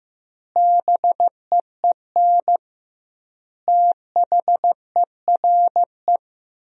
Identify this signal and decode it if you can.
Morse code, heard as a keyed tone, about 15 wpm, 700 Hz. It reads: BEEN THERE
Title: 2x Word-Spacing - Sentences from Top 400 Words - 15wpm